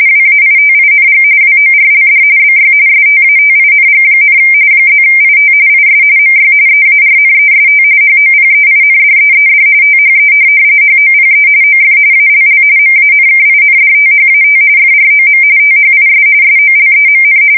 RTTY, about 45.45 baud. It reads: WAITING WORLD BY STATING, +IT IS MY EARNEST HOPEINDEED THE HOPE OF ALL MANKINDTHAT FROM THIS SOLEMN
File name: fsk170 / USS-Missouri